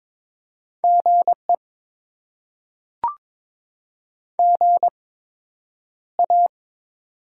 Morse code, heard as a keyed tone, about 22 wpm, spaced as for 9 wpm, 700 Hz. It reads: GE GA
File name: Top 20 QSO Elements - 8x Character Spacing - 22wpm